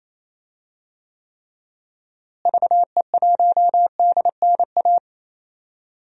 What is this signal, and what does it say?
Morse code, heard as a keyed tone, about 27 wpm, 700 Hz. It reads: VE1DNA
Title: DXCC-100 Call Signs - 6 Characters - Repeated 3x - 28wpm